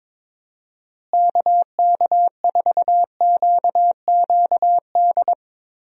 Morse code, heard as a keyed tone, about 22 wpm, 700 Hz. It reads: KK4QQD
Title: USA Call Signs - 6 Characters - Repeated 2x - 22wpm